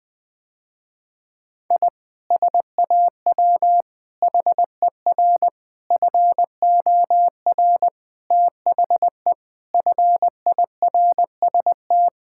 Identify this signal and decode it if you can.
Morse code, heard as a keyed tone, about 20 wpm, 700 Hz. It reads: I SAW HER FOR THE FIRST
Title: Sentences with 4-Word Phrases - Spoken Beginning - 20wpm